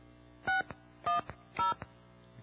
DTMF, heard as telephone touch tones, tones of about 150 ms. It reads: 610